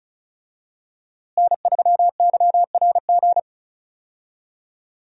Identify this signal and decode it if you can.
Morse code, heard as a keyed tone, about 34 wpm, 700 Hz. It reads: N3YRG
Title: USA Call Signs - 5 Characters - Repeated 3x - 35wpm